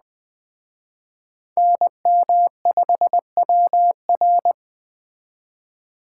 Morse code, heard as a keyed tone, about 21 wpm, 700 Hz. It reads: NM5WR